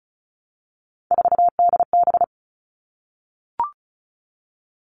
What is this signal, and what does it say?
Morse code, heard as a keyed tone, about 34 wpm, 700 Hz. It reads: 4DB
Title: All Cut Numbers - 3-Digit 35wpm